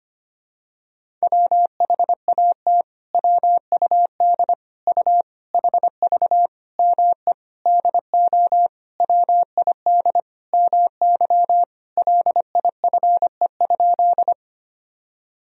Morse code, heard as a keyed tone, about 25 wpm, 700 Hz. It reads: WHAT WUD U HV ME DO WID MY LIFE?
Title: Sentences from Top 300 Words and Abbreviations 25wpm